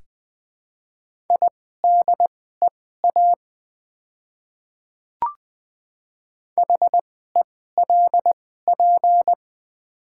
Morse code, heard as a keyed tone, about 20 wpm, spaced as for 15 wpm, 690 Hz. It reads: IDEA HELP
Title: 2x Character Spacing - Top 500 Words - 20wpm